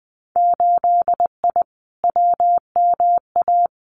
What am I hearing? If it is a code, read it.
Morse code, heard as a keyed tone, about 20 wpm, 700 Hz. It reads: 8I WMA
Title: Winter Field Day Exchanges Encore 20wpm